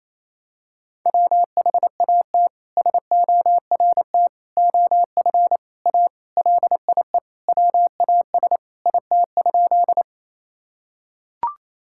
Morse code, heard as a keyed tone, about 28 wpm, 700 Hz. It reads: WHAT SORT OF A LIE WAS IT?
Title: Sentences from Top 1600 Words 28wpm